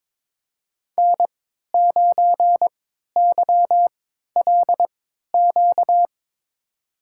Morse code, heard as a keyed tone, about 22 wpm, spaced as for 13 wpm, 700 Hz. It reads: N9YLQ
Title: USA Call Signs - 3x Character Spacing - 22wpm